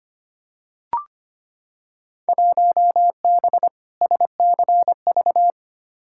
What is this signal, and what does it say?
Morse code, heard as a keyed tone, about 25 wpm, 700 Hz. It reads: 1B SCV
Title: Field Day Exchanges Encore - 25wpm